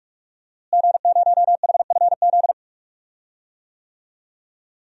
Morse code, heard as a keyed tone, about 44 wpm, 700 Hz. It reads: G0HFZ
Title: DXCC-100 Call Signs - 5 Characters - Repeated 3x - 45wpm